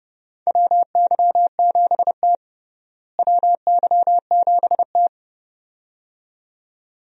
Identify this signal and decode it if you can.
Morse code, heard as a keyed tone, about 32 wpm, 700 Hz. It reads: WY7T WY7T
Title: USA Call Signs - 4 Characters - Repeated 2x - 30wpm